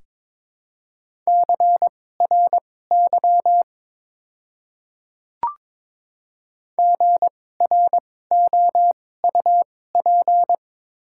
Morse code, heard as a keyed tone, about 22 wpm, spaced as for 16 wpm, 700 Hz. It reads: CRY GROUP
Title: 2x Character Spacing - Top 500 Words - 22wpm